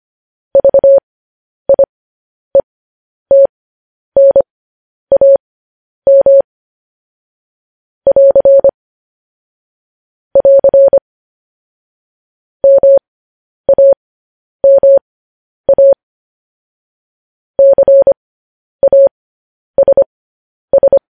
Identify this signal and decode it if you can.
Morse code, heard as a keyed tone, about 25 wpm, spaced as for 10 wpm, 550 Hz. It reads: VIETNAM <AR> <AR> MAMA CASS
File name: podcast / News headlines-1x-25x10